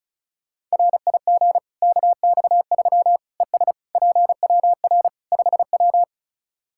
Morse code, heard as a keyed tone, about 36 wpm, 700 Hz. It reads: RIG KX3 ES PWR 5W
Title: Practice QSO - Exchange 2 - Info - 35wpm